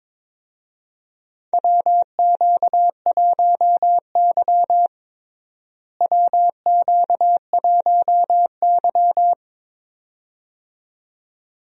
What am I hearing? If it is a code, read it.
Morse code, heard as a keyed tone, about 22 wpm, 700 Hz. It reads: WQ1Y WQ1Y
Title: USA Call Signs - 4 Characters - Repeated 2x - 22wpm